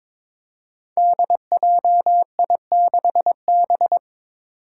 Morse code, heard as a keyed tone, about 22 wpm, 700 Hz. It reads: DJI6B
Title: Five Challenging Letter-Number Combinations 22wpm